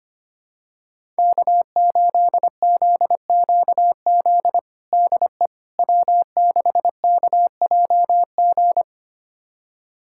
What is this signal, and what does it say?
Morse code, heard as a keyed tone, about 25 wpm, 700 Hz. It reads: K8ZQZ DE W6KJG
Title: USA Callsign DE Callsign - 5 Characters - 25wpm